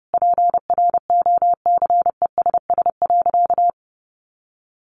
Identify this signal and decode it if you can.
Morse code, heard as a keyed tone, about 30 wpm, 700 Hz. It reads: PROCESS.
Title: Sentences from Top 1400 Words 30wpm